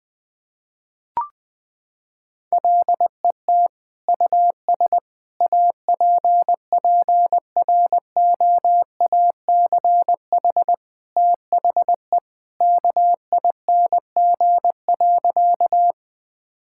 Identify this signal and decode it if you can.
Morse code, heard as a keyed tone, about 21 wpm, 700 Hz. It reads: LET US APPROACH THE KING.
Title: Sentences from Top 1000 Words - Review Part 6 - 20wpm